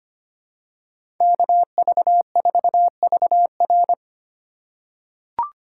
Morse code, heard as a keyed tone, about 25 wpm, 700 Hz. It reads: KV4VR
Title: USA Call Signs - 5 Characters - 3x Character Spacing - 25wpm